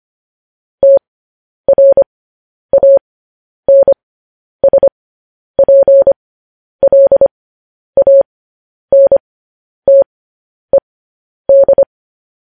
Morse code, heard as a keyed tone, about 25 wpm, spaced as for 10 wpm, 550 Hz. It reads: TRANSPLANTED
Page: news headlines in morse code at 10 wpm